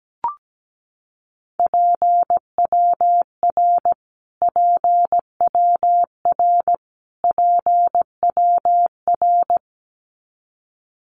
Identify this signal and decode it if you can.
Morse code, heard as a keyed tone, about 17 wpm, 700 Hz. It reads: PWR PWR PWR
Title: Abbreviations Repeated 3x - 17wpm